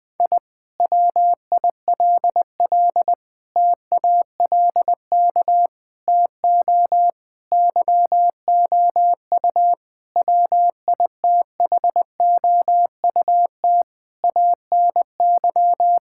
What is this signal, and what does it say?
Morse code, heard as a keyed tone, about 21 wpm, 700 Hz. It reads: I WILL TALK TO YOU WITHOUT ANY
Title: Sentences from Top 1000 Words - Review Part 6 - 20wpm